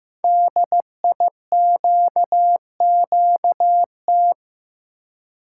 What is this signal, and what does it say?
Morse code, heard as a keyed tone, about 15 wpm, 700 Hz. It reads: DIQQT
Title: Code Groups - 5 Letters - 15wpm